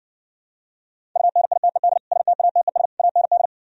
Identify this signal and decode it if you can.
Morse code, heard as a keyed tone, about 59 wpm, 700 Hz. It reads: UNITED STATES AND